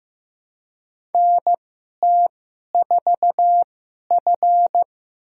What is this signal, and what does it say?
Morse code, heard as a keyed tone, about 15 wpm, spaced as for 11 wpm, 700 Hz. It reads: NT4F
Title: USA Call Signs - 4 Characters - 2x Character Spacing - 15wpm